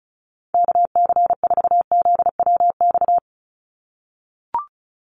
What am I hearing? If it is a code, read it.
Morse code, heard as a keyed tone, about 34 wpm, 700 Hz. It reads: KC4ZWX
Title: USA Call Signs 35wpm